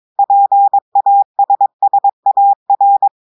Morse code, heard as a keyed tone, about 22 wpm, 800 Hz. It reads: PASSAR
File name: audios / portugais / portugais texte en clair 2 vitesse 22 mots minutes